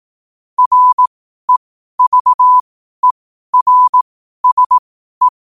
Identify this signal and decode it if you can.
Morse code, heard as a keyed tone, about 18 wpm, spaced as for 12 wpm, 1000 Hz. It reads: REVERSE
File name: qotdfiles / 12wpm